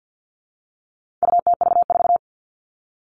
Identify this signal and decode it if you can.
Morse code, heard as a keyed tone, about 50 wpm, 700 Hz. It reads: UTV4
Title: All Cut Numbers - 4-Digit 50wpm